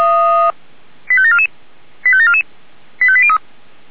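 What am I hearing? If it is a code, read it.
Five-tone selective call (ZVEI), heard as quick runs of five tones, tones of about 70 ms. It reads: 85744 85744 85803
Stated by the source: ZVEI five-tone call received on SDR.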